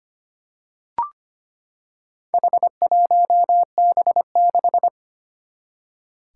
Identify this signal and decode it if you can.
Morse code, heard as a keyed tone, about 25 wpm, 700 Hz. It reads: H1B6
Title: Four Challenging Letter-Number Combinations 25wpm